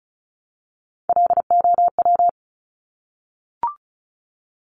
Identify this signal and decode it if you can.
Morse code, heard as a keyed tone, about 36 wpm, 700 Hz. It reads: LOW
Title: Top Three Letter Combinations in Words 35wpm